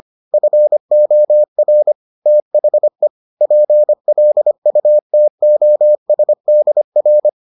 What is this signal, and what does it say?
Morse code, heard as a keyed tone, about 25 wpm, 600 Hz. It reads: FOR THE PLUTOSDR